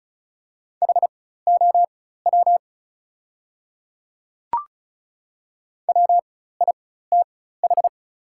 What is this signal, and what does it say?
Morse code, heard as a keyed tone, about 36 wpm, spaced as for 16 wpm, 700 Hz. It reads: HOW WITH
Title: 4x Character Spacing - Top 100 Words - 35wpm